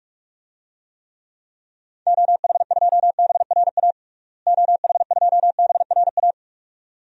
Morse code, heard as a keyed tone, about 46 wpm, 690 Hz. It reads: OH2BRA OH2BRA
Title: DXCC-100 Call Signs - 6 Characters - Repeated 2x - 45wpm